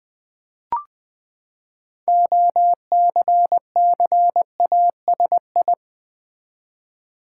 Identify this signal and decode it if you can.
Morse code, heard as a keyed tone, about 20 wpm, 700 Hz. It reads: OCCASI
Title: Build A Word - 6 Letters At A Time - 12-Letter Words - 20wpm